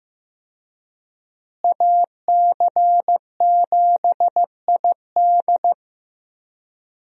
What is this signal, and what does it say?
Morse code, heard as a keyed tone, about 15 wpm, 700 Hz. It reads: AC7ID
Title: USA Call Signs - 5 Characters - Repeated 2x - 15wpm